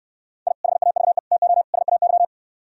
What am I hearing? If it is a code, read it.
Morse code, heard as a keyed tone, about 68 wpm, 700 Hz. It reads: I HAVE TO SAY